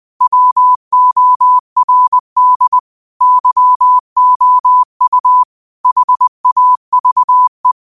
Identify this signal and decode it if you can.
Morse code, heard as a keyed tone, about 20 wpm, 1000 Hz. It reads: WORD YOU HAVE
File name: qotdfiles / 20wpm